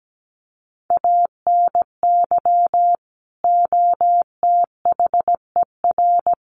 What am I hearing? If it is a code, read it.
Morse code, heard as a keyed tone, about 17 wpm, 700 Hz. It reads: ANY OTHER
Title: Sentences from Top 1200 Words 17wpm